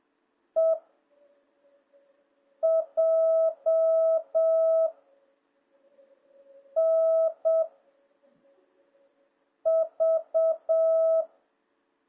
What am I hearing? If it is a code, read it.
Morse code, heard as a keyed tone, about 7 wpm, 660 Hz. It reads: E J N V